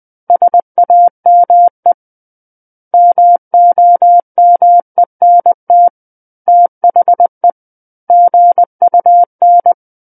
Morse code, heard as a keyed tone, about 20 wpm, 700 Hz. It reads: SAME MOMENT THE GUN
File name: 20 WPM